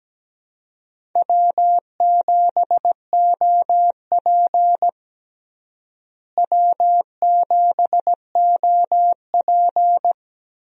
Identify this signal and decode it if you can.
Morse code, heard as a keyed tone, about 17 wpm, 700 Hz. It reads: W7OP W7OP
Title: USA Call Signs - 4 Characters - Repeated 3x - 17wpm